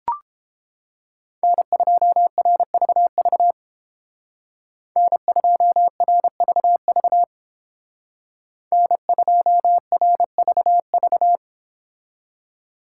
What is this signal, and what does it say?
Morse code, heard as a keyed tone, about 29 wpm, 700 Hz. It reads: N2RVV N2RVV N2RVV